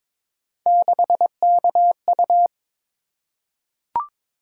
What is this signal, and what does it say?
Morse code, heard as a keyed tone, about 22 wpm, 700 Hz. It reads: 6KU
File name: practice / 22wpm / Three Letter-Number Combinations - 22wpm